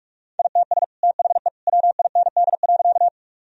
Morse code, heard as a keyed tone, about 46 wpm, 700 Hz. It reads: ITS THE WIND.
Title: Sentences from Top 1000 Words - Review Part 2 - 45wpm